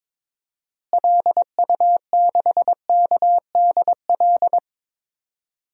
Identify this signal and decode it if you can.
Morse code, heard as a keyed tone, about 21 wpm, 700 Hz. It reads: LU6KDL
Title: DXCC-100 Call Signs - 6 Characters - Repeated 3x - 22wpm